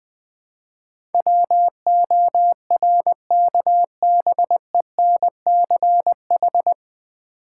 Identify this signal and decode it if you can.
Morse code, heard as a keyed tone, about 21 wpm, 690 Hz. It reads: WORKBENCH
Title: Difficult Words with Hints - 20wpm